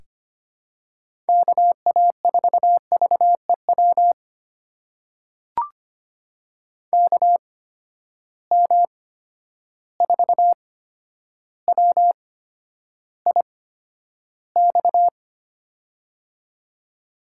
Morse code, heard as a keyed tone, about 25 wpm, spaced as for 11 wpm, 700 Hz. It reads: KA4VEW KM4WIX
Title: USA Call Signs - 6 Characters - 8x Character Spacing - 25wpm